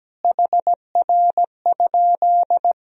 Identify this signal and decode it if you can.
Morse code, heard as a keyed tone, about 17 wpm, 700 Hz. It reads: HR?
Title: Sentences from Top 300 Words and Abbreviations 17wpm